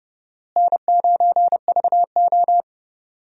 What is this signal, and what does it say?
Morse code, heard as a keyed tone, about 32 wpm, 700 Hz. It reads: N9VO
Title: USA Call Signs - 4 Characters - Repeated 2x - 30wpm